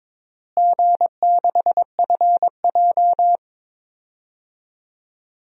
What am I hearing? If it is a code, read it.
Morse code, heard as a keyed tone, about 22 wpm, 700 Hz. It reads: G6FJ
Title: DXCC-100 Call Signs - 4 Characters - Repeated 3x - 22wpm